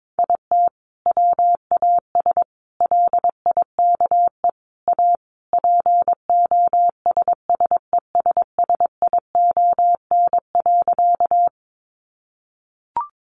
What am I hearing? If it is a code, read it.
Morse code, heard as a keyed tone, about 22 wpm, 700 Hz. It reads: IT WAS LIKE A POSSESSION.
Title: Sentences from Top 1400 Words 22wpm